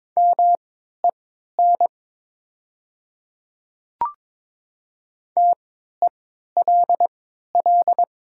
Morse code, heard as a keyed tone, about 22 wpm, spaced as for 12 wpm, 700 Hz. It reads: MEN TELL
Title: 3x Character Spacing - Top 300 Words - 22wpm